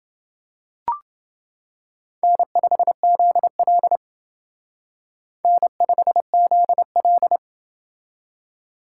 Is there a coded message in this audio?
Yes: Morse code from 2.2 to 7.4 s, heard as a keyed tone, about 28 wpm, 700 Hz. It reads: N5ZL N5ZL